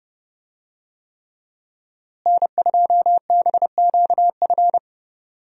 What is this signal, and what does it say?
Morse code, heard as a keyed tone, about 30 wpm, 700 Hz. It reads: N2BQF
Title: USA Call Signs - 5 Characters - Repeated 3x - 30wpm